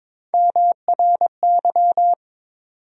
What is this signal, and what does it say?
Morse code, heard as a keyed tone, about 22 wpm, 700 Hz. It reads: MRY
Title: USA Call Sign Postfixes - 3 Letters 22wpm